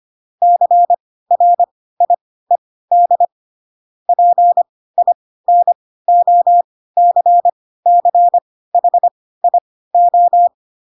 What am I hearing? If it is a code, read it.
Morse code, heard as a keyed tone, about 25 wpm, spaced as for 16 wpm, 700 Hz. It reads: CRIED PINOCCHIO
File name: Pinocchio / Pinocchio ch7 10 25 ik1hge